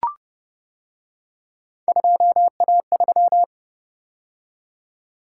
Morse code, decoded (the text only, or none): 2A3